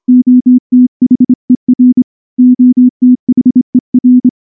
Morse code, heard as a keyed tone, about 25 wpm, 250 Hz. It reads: OTHER OTHER
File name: morse / morse-training-25WPM